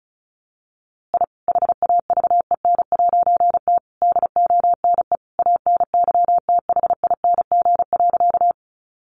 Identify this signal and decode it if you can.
Morse code, heard as a keyed tone, about 34 wpm, 700 Hz. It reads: I HAVEN'T DONE ANYTHING.
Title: A Sound Of Thunder 35wpm